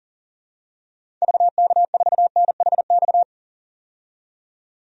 Morse code, heard as a keyed tone, about 37 wpm, 700 Hz. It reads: VK4NHX
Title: DXCC-100 Call Signs - 6 Characters - 40wpm